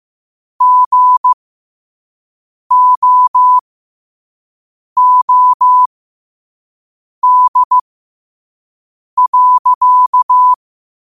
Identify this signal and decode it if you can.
Morse code, heard as a keyed tone, about 15 wpm, spaced as for 5 wpm, 1000 Hz. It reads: GOOD.
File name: qotdfiles / 5wpm